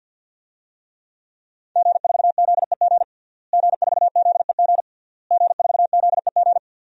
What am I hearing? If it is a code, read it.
Morse code, heard as a keyed tone, about 52 wpm, 700 Hz. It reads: G4ZEG G4ZEG G4ZEG
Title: DXCC-100 Call Signs - 5 Characters - Repeated 3x - 50wpm